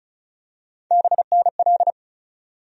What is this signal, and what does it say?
Morse code, heard as a keyed tone, about 36 wpm, 700 Hz. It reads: BNL
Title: USA Call Sign Postfixes - 3 Letters 35wpm